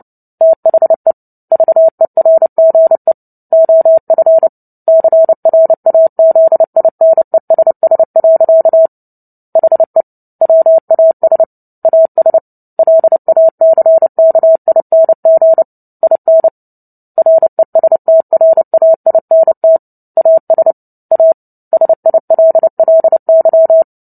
Morse code, heard as a keyed tone, about 30 wpm, 650 Hz. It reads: THE VERGE OF CRAZINESS. HE WAS AS LACKING IN RESTRAINT AS A SILLY
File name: Book Two / 03-Chapter 3 - 29 WPM